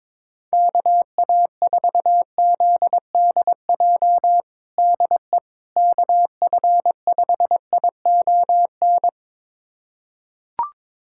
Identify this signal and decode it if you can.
Morse code, heard as a keyed tone, about 22 wpm, 700 Hz. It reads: KA4ZDJ DE KF5ION
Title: USA Callsign DE Callsign - 6 Characters - 22wpm